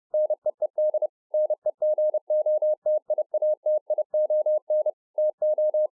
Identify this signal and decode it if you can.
Morse code, heard as a keyed tone, about 30 wpm, 610 Hz. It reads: NEED NEGOTIATION TO